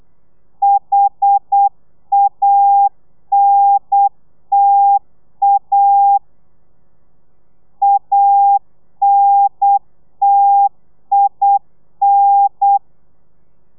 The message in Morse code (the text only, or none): HANTA ANTIN